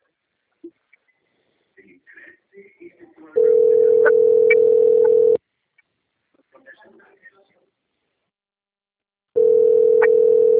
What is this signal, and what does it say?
Ringback tone (North America), 440 + 480 Hz.